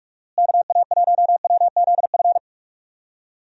Morse code, heard as a keyed tone, about 43 wpm, 700 Hz. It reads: KA1WZF
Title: USA Call Signs - 6 Characters - Repeated 3x - 45wpm